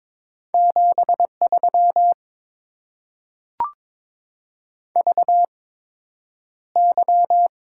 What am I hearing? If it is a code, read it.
Morse code, heard as a keyed tone, about 22 wpm, spaced as for 9 wpm, 700 Hz. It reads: 73 VY
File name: Top 90 QSO Elements - 8x Character Spacing - 22wpm